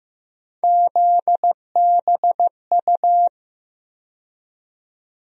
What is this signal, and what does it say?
Morse code, heard as a keyed tone, about 15 wpm, 700 Hz. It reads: ZBU
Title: USA Call Sign Postfixes - 3 Letters 15wpm